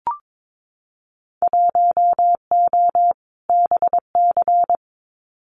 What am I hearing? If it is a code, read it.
Morse code, heard as a keyed tone, about 22 wpm, 700 Hz. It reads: 1O BC
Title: Winter Field Day Exchanges Encore 22wpm